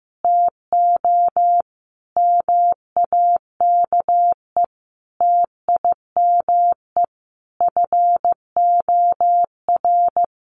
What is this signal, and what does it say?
Morse code, heard as a keyed tone, about 15 wpm, 700 Hz. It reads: TO MAKE TIME FOR
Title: Sentences from Top 1100 Words 15wpm